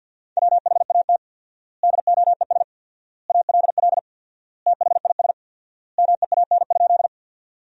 Morse code, heard as a keyed tone, about 50 wpm, 700 Hz. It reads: WHAT DOES ALL THIS MEAN?
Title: 4x Word-Spacing - Sentences from Top 200 Words - 50wpm